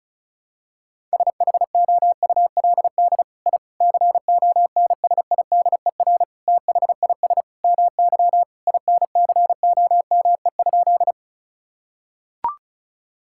Morse code, heard as a keyed tone, about 36 wpm, 700 Hz. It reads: SHOULD I CONSIDER THIS MY INCOME?
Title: Sentences from Top 1000 Words - Review Part 5 - 35wpm